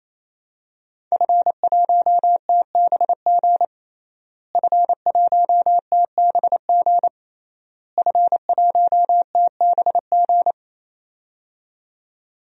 Morse code, heard as a keyed tone, about 28 wpm, 700 Hz. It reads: F1TBG F1TBG F1TBG
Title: DXCC-100 Call Signs - 5 Characters - Repeated 3x - 28wpm